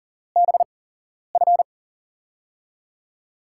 Morse code, heard as a keyed tone, about 40 wpm, spaced as for 11 wpm, 700 Hz. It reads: BF
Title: USA Call Signs - 4 Characters - 8x Character Spacing - 40wpm